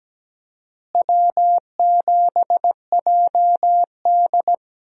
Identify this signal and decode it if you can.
Morse code, heard as a keyed tone, about 17 wpm, 700 Hz. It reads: W7JD
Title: USA Call Signs - 4 Characters - Repeated 3x - 17wpm